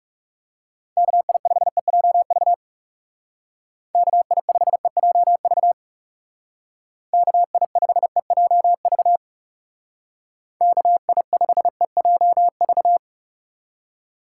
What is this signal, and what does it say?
Morse code, heard as a keyed tone, about 36 wpm, 700 Hz. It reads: KI5EJV KI5EJV KI5EJV KI5EJV
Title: Speed Racing - USA Call Signs 30wpm